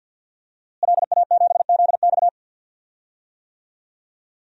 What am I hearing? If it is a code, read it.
Morse code, heard as a keyed tone, about 49 wpm, 700 Hz. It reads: LA7ZX
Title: DXCC-100 Call Signs - 5 Characters - Repeated 2x - 50wpm